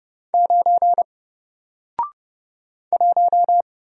Morse code, heard as a keyed tone, about 31 wpm, 700 Hz. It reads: 9 1
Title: Single Letter-Number 30wpm